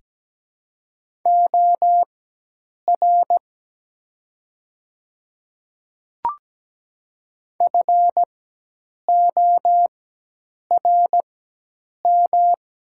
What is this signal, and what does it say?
Morse code, heard as a keyed tone, about 17 wpm, spaced as for 8 wpm, 700 Hz. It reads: OR FORM
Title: 4x Character Spacing - Top 200 Words - 17wpm